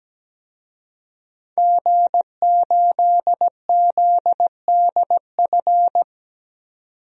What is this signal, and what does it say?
Morse code, heard as a keyed tone, about 17 wpm, 700 Hz. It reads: G8ZDF
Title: DXCC-100 Call Signs - 5 Characters - 17wpm